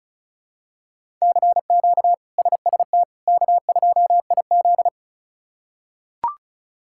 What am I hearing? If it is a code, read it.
Morse code, heard as a keyed tone, about 36 wpm, 700 Hz. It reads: CQ SST K2IZ
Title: K1USN SST QSO Practice - 35wpm